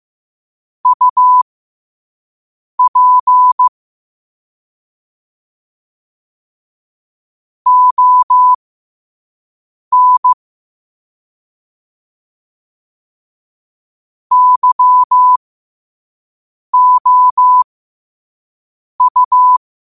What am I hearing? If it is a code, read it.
Morse code, heard as a keyed tone, about 15 wpm, spaced as for 5 wpm, 1000 Hz. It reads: UP ON YOU
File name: qotdfiles / 5wpm